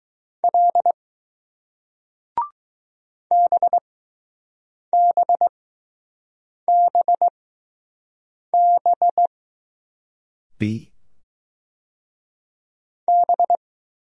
Morse code, heard as a keyed tone, about 20 wpm, spaced as for 6 wpm, 700 Hz. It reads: L BBBB B